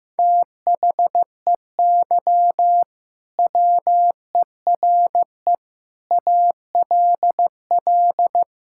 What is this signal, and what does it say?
Morse code, heard as a keyed tone, about 15 wpm, 700 Hz. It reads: THEY WERE ALL
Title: Sentences from Top 1600 Words 15wpm